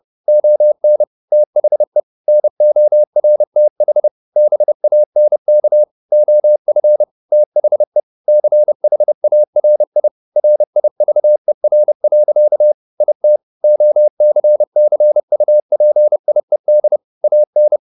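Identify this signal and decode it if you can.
Morse code, heard as a keyed tone, about 30 wpm, 600 Hz. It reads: ON THE NORTH BANK OF THE CHARI RIVER. IT OCCUPIED AN
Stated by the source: Morse code at 29 wpm, sending Ken Follett's novel Never in English, from CW med Gnister.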